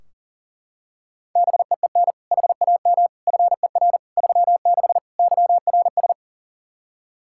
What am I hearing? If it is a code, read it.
Morse code, heard as a keyed tone, about 40 wpm, 700 Hz. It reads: BEEN HAM FER 36 YRS
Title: Practice QSO - Exchange 3 - More Info - 40wpm